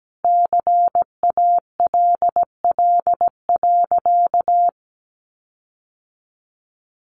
Morse code, heard as a keyed tone, about 17 wpm, 700 Hz. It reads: CALL.
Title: Sentences from Top 1300 Words 17wpm